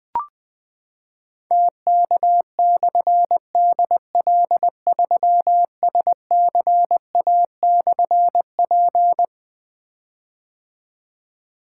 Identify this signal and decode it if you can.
Morse code, heard as a keyed tone, about 21 wpm, 700 Hz. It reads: TK/DL3SCA/P
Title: Difficult Call Signs 20wpm